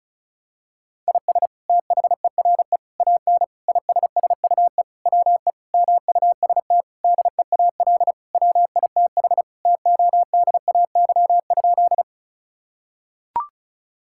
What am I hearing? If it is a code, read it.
Morse code, heard as a keyed tone, about 34 wpm, 700 Hz. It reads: IS THERE AN ISSUE WE MUST DEAL WITH TODAY?
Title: Sentences from Top 1000 Words - Review Part 6 - 35wpm